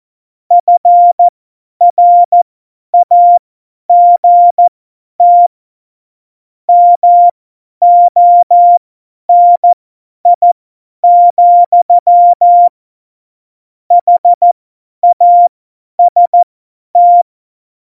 Morse code, heard as a keyed tone, about 14 wpm, spaced as for 10 wpm, 700 Hz. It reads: FRAGT MONI, HAST
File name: Morsekurs / Witze / Witz 19. November 2019, 50 F70